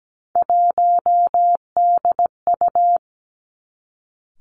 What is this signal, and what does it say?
Morse code, heard as a keyed tone, about 17 wpm, 700 Hz. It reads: 1DU
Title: Three Challenging Letter-Number Combinations 17wpm